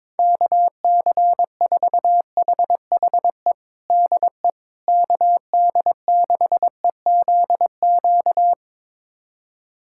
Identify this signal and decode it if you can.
Morse code, heard as a keyed tone, about 22 wpm, 700 Hz. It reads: KC4HHE DE KD6EZQ
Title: USA Callsign DE Callsign - 6 Characters - 22wpm